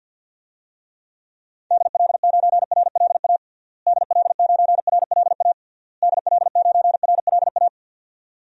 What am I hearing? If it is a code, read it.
Morse code, heard as a keyed tone, about 51 wpm, 700 Hz. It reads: DL9RLA DL9RLA DL9RLA
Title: DXCC-100 Call Signs - 6 Characters - Repeated 3x - 50wpm